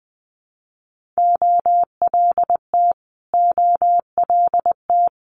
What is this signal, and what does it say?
Morse code, heard as a keyed tone, about 20 wpm, 700 Hz. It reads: OLT OLT
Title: Multi-letter Phonemes Repeated 3x 20wpm